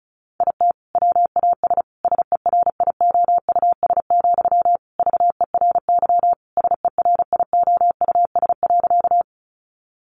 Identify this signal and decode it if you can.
Morse code, heard as a keyed tone, about 34 wpm, 700 Hz. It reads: IT WAS SERIOUS, VERY SERIOUS.
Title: Sentences from Top 1100 Words 35wpm